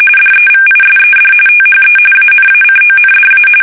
RTTY, about 45.45 baud. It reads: DEVISED THE ANACONDA
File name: fsk850 / usnavy3